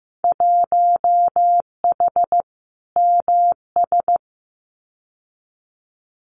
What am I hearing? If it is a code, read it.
Morse code, heard as a keyed tone, about 15 wpm, 700 Hz. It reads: 1H MS
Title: Winter Field Day Exchanges 15wpm